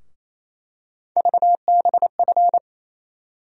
Morse code, heard as a keyed tone, about 28 wpm, 700 Hz. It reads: VBF
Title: USA Call Sign Postfixes - 3 Letters 28wpm